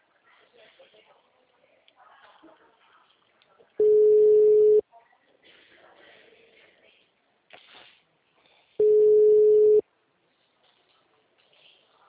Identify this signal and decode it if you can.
Ringback tone (Europe, 425 Hz), with speech.
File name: out- -2030- - - .22